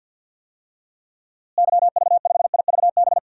Morse code, heard as a keyed tone, about 52 wpm, 700 Hz. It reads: YV5IVB
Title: DXCC-100 Call Signs - 6 Characters - Repeated 3x - 50wpm